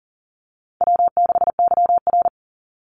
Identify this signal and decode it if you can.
Morse code, heard as a keyed tone, about 40 wpm, 700 Hz. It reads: W6YR
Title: Easy Call Signs - 40wpm